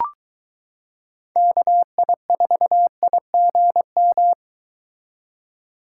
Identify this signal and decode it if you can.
Morse code, heard as a keyed tone, about 23 wpm, 700 Hz. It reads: KI4IGM